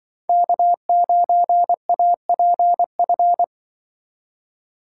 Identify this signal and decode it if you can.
Morse code, heard as a keyed tone, about 24 wpm, 700 Hz. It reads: K9APF
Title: Speed Racing - USA 1x3 Call Signs 20wpm